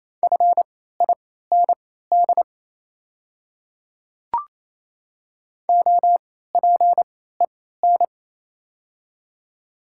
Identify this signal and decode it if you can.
Morse code, heard as a keyed tone, about 27 wpm, spaced as for 16 wpm, 700 Hz. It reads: FIND OPEN
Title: 3x Character Spacing - Top 300 Words - 28wpm